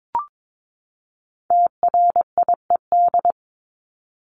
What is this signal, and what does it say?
Morse code, heard as a keyed tone, about 22 wpm, 700 Hz. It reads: TRIED
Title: ED Suffix - Top 100 Words with Base Words 22wpm